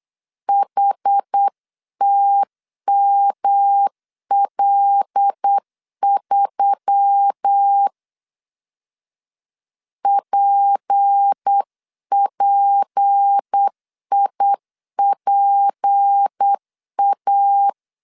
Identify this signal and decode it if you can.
Morse code, heard as a keyed tone, about 8 wpm, 800 Hz. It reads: HTML3 PPIPA
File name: LEZIONI / 6 PKX0 / 6 P 8wpm 3025